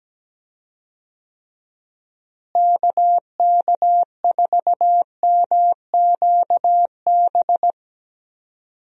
Morse code, heard as a keyed tone, about 17 wpm, 700 Hz. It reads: KK4MQB
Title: USA Call Signs - 6 Characters - Repeated 2x - 17wpm